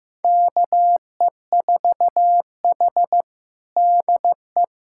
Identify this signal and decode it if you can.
Morse code, heard as a keyed tone, about 15 wpm, 700 Hz. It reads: KE4H DE
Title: USA Callsign DE Callsign - 4 Characters - 15wpm